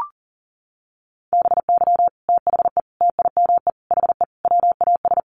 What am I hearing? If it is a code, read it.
Morse code, heard as a keyed tone, about 40 wpm, 700 Hz. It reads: BY THE TIME HE WAS
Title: Sets of 5 Words Encore 40wpm